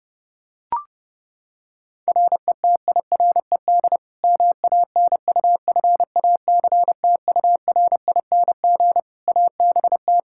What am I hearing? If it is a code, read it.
Morse code, heard as a keyed tone, about 30 wpm, 700 Hz. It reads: RETIRED MANUFACTURING ABT
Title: K1USN SST QSO Practice 30wpm